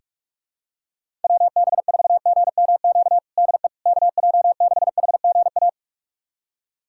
Morse code, heard as a keyed tone, about 46 wpm, 700 Hz. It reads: WB4GMQ DE KJ6HGA
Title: USA Callsign DE Callsign - 6 Characters - 45wpm